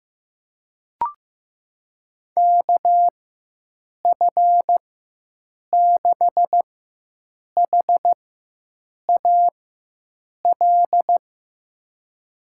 Morse code, heard as a keyed tone, about 15 wpm, spaced as for 7 wpm, 700 Hz. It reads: KF6HAL
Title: USA Call Signs - 4x Character Spacing - 15wpm